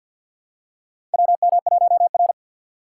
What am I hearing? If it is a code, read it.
Morse code, heard as a keyed tone, about 52 wpm, 700 Hz. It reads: WM1R
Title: USA Call Signs - 4 Characters - 6x Character Spacing - 50wpm